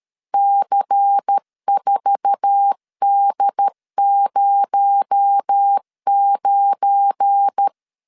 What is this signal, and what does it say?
Morse code, heard as a keyed tone, about 13 wpm, 800 Hz. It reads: C4D09
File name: LEZIONI / 8 YJ45 / 8 YJ4 12wpm 2151